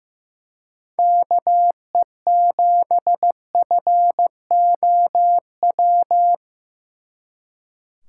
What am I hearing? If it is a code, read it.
Morse code, heard as a keyed tone, about 18 wpm, 700 Hz. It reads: KE7FOW
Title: Speed Racing - USA 2x3 Call Signs 15wpm